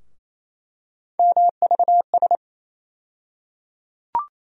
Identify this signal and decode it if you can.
Morse code, heard as a keyed tone, about 28 wpm, 700 Hz. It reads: MVS
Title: USA Call Sign Postfixes - 3 Letters 28wpm